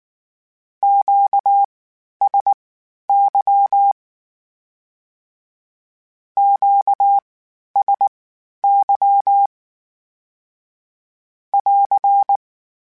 Morse code, heard as a keyed tone, about 19 wpm, spaced as for 11 wpm, 800 Hz. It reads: QSY QSY <AR>
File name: Q Calls Twice sp @18